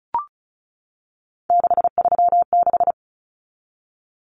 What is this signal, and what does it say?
Morse code, heard as a keyed tone, about 34 wpm, 700 Hz. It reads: 636
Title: Numbers - 3-Digit 35wpm